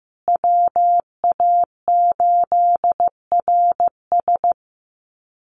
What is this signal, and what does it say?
Morse code, heard as a keyed tone, about 14 wpm, 700 Hz. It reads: WA8RS
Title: USA Call Signs Repeated 3x 15wpm